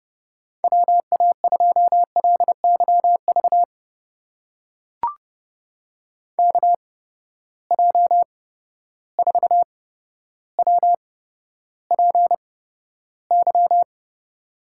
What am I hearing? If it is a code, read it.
Morse code, heard as a keyed tone, about 30 wpm, spaced as for 13 wpm, 700 Hz. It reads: WA2LYV KJ4WPY
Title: USA Call Signs - 8x Character Spacing - 30wpm